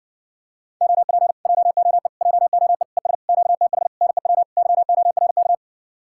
Morse code, heard as a keyed tone, about 52 wpm, 700 Hz. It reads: OP JOE JOE ES QTH NEW YORK